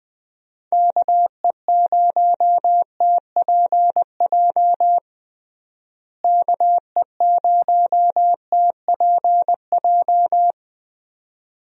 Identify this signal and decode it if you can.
Morse code, heard as a keyed tone, about 20 wpm, 700 Hz. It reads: KE0TPJ KE0TPJ
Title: USA Call Signs - 6 Characters - Repeated 2x - 20wpm